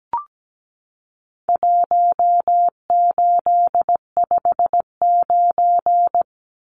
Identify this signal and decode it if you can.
Morse code, heard as a keyed tone, about 17 wpm, 700 Hz. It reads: 1859
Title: Numbers - 4-Digit 17wpm